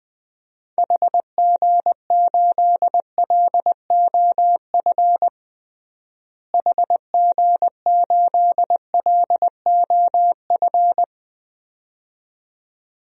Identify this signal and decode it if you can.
Morse code, heard as a keyed tone, about 20 wpm, 700 Hz. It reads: HG8LOF HG8LOF
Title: DXCC-100 Call Signs - 6 Characters - Repeated 2x - 20wpm